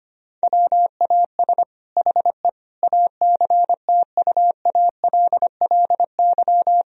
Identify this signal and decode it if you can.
Morse code, heard as a keyed tone, about 25 wpm, 700 Hz. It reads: WAS HE ACTUALLY
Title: Sentences from Top 1000 Words - Review Part 4 - 25wpm